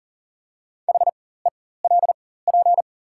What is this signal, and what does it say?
Morse code, heard as a keyed tone, about 38 wpm, spaced as for 18 wpm, 700 Hz. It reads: HELP
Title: 4x Character Spacing - Top 500 Words - 40wpm